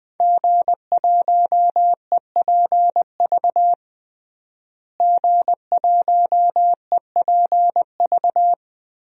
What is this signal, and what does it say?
Morse code, heard as a keyed tone, about 21 wpm, 700 Hz. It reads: G1EPV G1EPV
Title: DXCC-100 Call Signs - 5 Characters - Repeated 2x - 20wpm